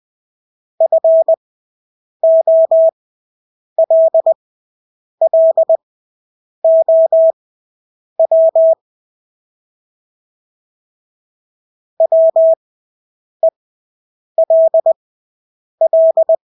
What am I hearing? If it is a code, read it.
Morse code, heard as a keyed tone, about 20 wpm, spaced as for 8 wpm, 650 Hz. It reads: FOLLOW WELL
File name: Engelske ord 20/8 wpm